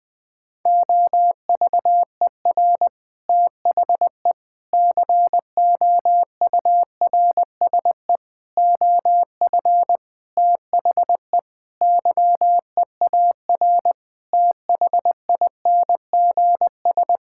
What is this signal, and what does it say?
Morse code, heard as a keyed tone, about 21 wpm, 700 Hz. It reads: OVER THE COURSE OF THE YEAR THINGS
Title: Sentences with 5-Word Phrases - 20wpm